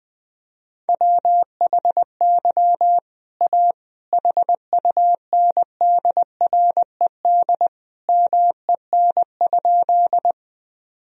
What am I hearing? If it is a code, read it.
Morse code, heard as a keyed tone, about 21 wpm, 700 Hz. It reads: WHY A HUNDRED MEN?
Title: Sentences from Top 1000 Words - Review Part 2 - 20wpm